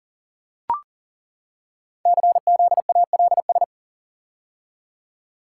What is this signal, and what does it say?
Morse code, heard as a keyed tone, about 42 wpm, 700 Hz. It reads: CZALS
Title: Code Groups - 5 Letters - 40wpm